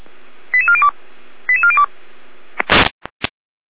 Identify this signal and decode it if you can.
Five-tone selective call (ZVEI), heard as quick runs of five tones, tones of about 70 ms. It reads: 88402 88402